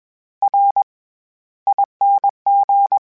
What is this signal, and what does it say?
Morse code, heard as a keyed tone, about 21 wpm, spaced as for 14 wpm, 800 Hz. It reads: RING
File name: Special INGs-mp3s / Special INGs-mp3s / SlowToFaster ing @20